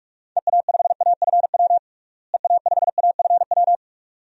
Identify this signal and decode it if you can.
Morse code, heard as a keyed tone, about 46 wpm, 700 Hz. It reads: EA5AFW EA5AFW
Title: DXCC-100 Call Signs - 6 Characters - Repeated 2x - 45wpm